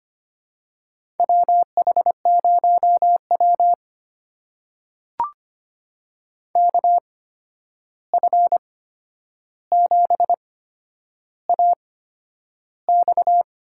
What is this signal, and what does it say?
Morse code, heard as a keyed tone, about 25 wpm, spaced as for 10 wpm, 700 Hz. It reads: WH0W KF7AX
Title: USA Call Signs - 8x Character Spacing - 25wpm